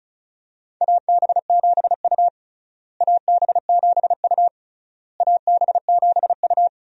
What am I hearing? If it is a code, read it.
Morse code, heard as a keyed tone, about 36 wpm, 700 Hz. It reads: AB7U AB7U AB7U
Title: USA Call Signs - 4 Characters - Repeated 3x - 35wpm